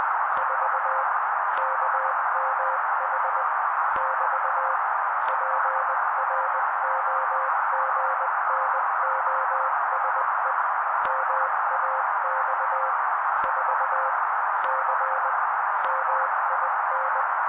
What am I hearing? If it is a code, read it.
Morse code, heard as a keyed tone, about 20 wpm, 530 Hz. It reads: 4 KMH <BT> PROGNOSE MAX 4 C MIN
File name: daily / 18wpm noise